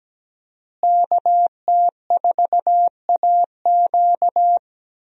Morse code, heard as a keyed tone, about 17 wpm, 700 Hz. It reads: KT4AQ
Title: USA Call Signs - 5 Characters - Repeated 3x - 17wpm